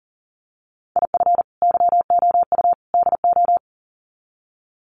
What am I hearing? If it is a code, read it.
Morse code, heard as a keyed tone, about 40 wpm, 700 Hz. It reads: IF YOU DO
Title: Sets of 3 Words 40wpm